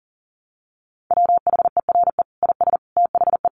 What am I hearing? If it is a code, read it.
Morse code, heard as a keyed tone, about 40 wpm, 700 Hz. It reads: WHERE IS THE
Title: Sentences from Top 1200 Words 40wpm